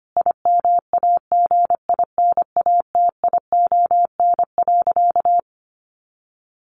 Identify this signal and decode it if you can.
Morse code, heard as a keyed tone, about 25 wpm, 700 Hz. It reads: IMAGINATION.
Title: Sentences from Top 1300 Words 25wpm